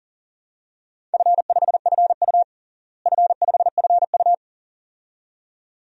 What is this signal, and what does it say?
Morse code, heard as a keyed tone, about 40 wpm, 700 Hz. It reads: F5FU F5FU
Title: DXCC-100 Call Signs - 4 Characters - Repeated 2x - 40wpm